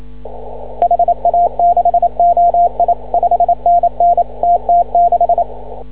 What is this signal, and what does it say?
Morse code, heard as a keyed tone, about 27 wpm, 700 Hz. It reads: HA6OI 5NN TT6
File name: CW / ha6oi 2